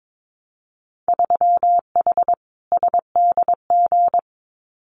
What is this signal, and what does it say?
Morse code, heard as a keyed tone, about 22 wpm, 700 Hz. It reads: 3H SDG
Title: Winter Field Day Exchanges Encore 22wpm